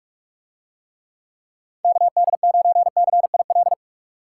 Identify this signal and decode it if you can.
Morse code, heard as a keyed tone, about 44 wpm, 700 Hz. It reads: KD9CIL